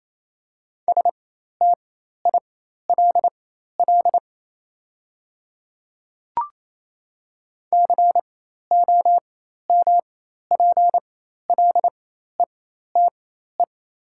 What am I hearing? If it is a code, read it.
Morse code, heard as a keyed tone, about 28 wpm, spaced as for 13 wpm, 700 Hz. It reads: STILL COMPLETE
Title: 4x Character Spacing - Top 500 Words - 28wpm